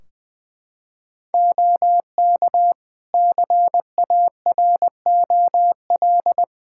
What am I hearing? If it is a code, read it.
Morse code, heard as a keyed tone, about 20 wpm, 700 Hz. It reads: OK CAROL
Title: Practice QSO - Exchange 4 - Ending - 20wpm